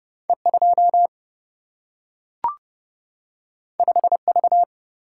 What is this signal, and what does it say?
Morse code, heard as a keyed tone, about 30 wpm, 700 Hz. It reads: E2 5V
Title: Single-Letter Single-Number 30wpm